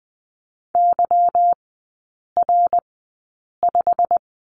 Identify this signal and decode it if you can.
Morse code, heard as a keyed tone, about 20 wpm, spaced as for 8 wpm, 700 Hz. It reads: YR5
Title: 191 Lesson 31e - Letter v - Three Characters 20wpm